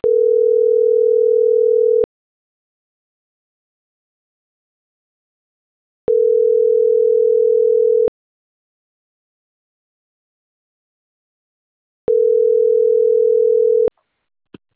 Ringback tone (North America), 440 + 480 Hz.